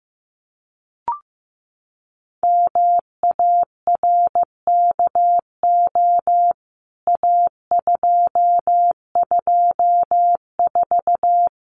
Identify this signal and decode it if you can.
Morse code, heard as a keyed tone, about 15 wpm, 700 Hz. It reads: MARKO A224
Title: CWT Exchange 15wpm